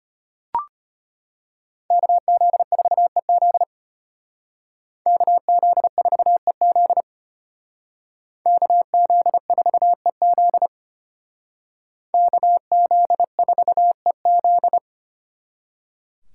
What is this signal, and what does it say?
Morse code, heard as a keyed tone, about 32 wpm, 700 Hz. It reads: KZ4EZ KZ4EZ KZ4EZ KZ4EZ